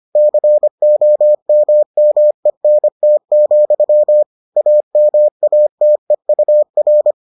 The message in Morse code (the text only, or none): COMMENT, AMATEUR